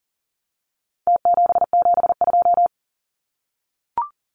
Morse code, heard as a keyed tone, about 37 wpm, 700 Hz. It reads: T772
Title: Numbers - Contest Style - 4-Digit 40wpm